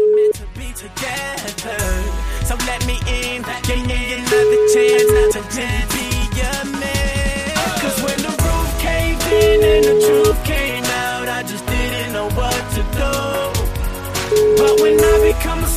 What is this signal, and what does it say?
Ringback tone (Europe, 425 Hz), with noise.